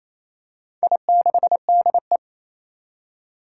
Morse code, heard as a keyed tone, about 28 wpm, 700 Hz. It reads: I6DE
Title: Four Challenging Letter-Number Combinations 28wpm